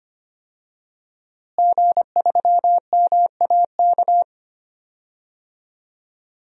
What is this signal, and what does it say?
Morse code, heard as a keyed tone, about 25 wpm, 700 Hz. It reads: G3MAK
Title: DXCC-100 Call Signs - 5 Characters - Repeated 2x - 25wpm